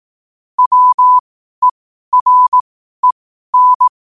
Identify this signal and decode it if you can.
Morse code, heard as a keyed tone, about 18 wpm, spaced as for 12 wpm, 1000 Hz. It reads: WEREN
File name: qotdfiles / 12wpm